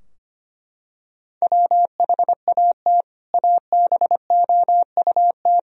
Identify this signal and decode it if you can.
Morse code, heard as a keyed tone, about 25 wpm, 700 Hz. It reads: WHAT ABOUT
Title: Sentences from Top 1000 Words - Review Part 2 - 25wpm